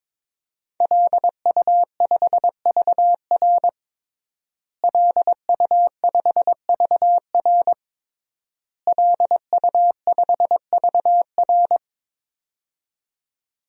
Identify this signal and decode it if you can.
Morse code, heard as a keyed tone, about 22 wpm, 700 Hz. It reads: LU5VR LU5VR LU5VR
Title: DXCC-100 Call Signs - 5 Characters - Repeated 3x - 22wpm